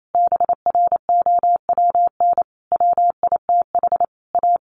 Morse code, heard as a keyed tone, about 28 wpm, 700 Hz. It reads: BROWN WITH A
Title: Sentences from Top 1400 Words 28wpm